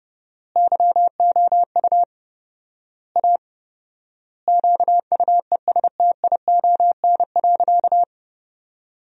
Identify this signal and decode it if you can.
Morse code, heard as a keyed tone, about 30 wpm, 700 Hz. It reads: YOU A QUESTION.
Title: 4x Word-Spacing - Sentences from Top 400 Words - 30wpm